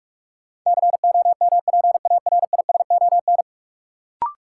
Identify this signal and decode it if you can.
Morse code, heard as a keyed tone, about 42 wpm, 700 Hz. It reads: COMPARISON
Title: Sentences from Top 1600 Words - New Word Drill Alt - 45wpm